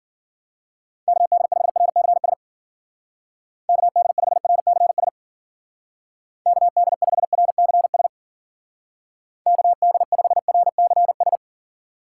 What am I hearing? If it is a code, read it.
Morse code, heard as a keyed tone, about 49 wpm, 700 Hz. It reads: KD5RCS KD5RCS KD5RCS KD5RCS
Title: Speed Racing - USA 2x3 Call Signs 40wpm